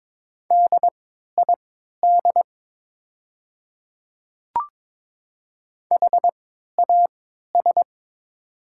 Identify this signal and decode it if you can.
Morse code, heard as a keyed tone, about 22 wpm, spaced as for 12 wpm, 700 Hz. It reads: DID HAS
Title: 3x Character Spacing - Top 100 Words - 22wpm